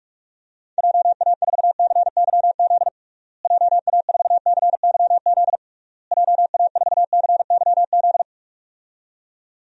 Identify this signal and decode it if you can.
Morse code, heard as a keyed tone, about 46 wpm, 700 Hz. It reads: JA4CYZ JA4CYZ JA4CYZ
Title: DXCC-100 Call Signs - 6 Characters - Repeated 3x - 45wpm